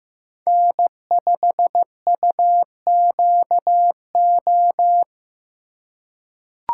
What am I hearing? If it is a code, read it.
Morse code, heard as a keyed tone, about 15 wpm, 700 Hz. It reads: N5UQO
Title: USA Call Signs - 5 Characters - 2x Character Spacing - 15wpm